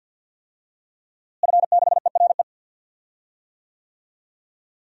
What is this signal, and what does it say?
Morse code, heard as a keyed tone, about 51 wpm, 700 Hz. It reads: F6ERE
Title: DXCC-100 Call Signs - 5 Characters - Repeated 3x - 50wpm